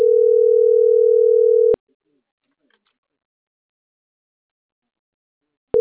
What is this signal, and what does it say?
Ringback tone (North America), 440 + 480 Hz.